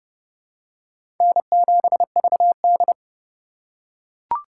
Morse code, heard as a keyed tone, about 32 wpm, 700 Hz. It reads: N7VD
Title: USA Call Signs - 4 Characters - 2x Character Spacing - 30wpm